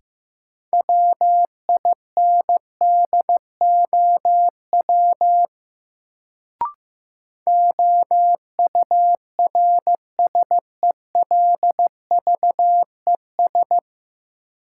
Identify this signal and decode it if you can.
Morse code, heard as a keyed tone, about 15 wpm, 700 Hz. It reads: WINDOW OURSELVES
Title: Sentences from Top 1600 Words - New Word Drill - 15wpm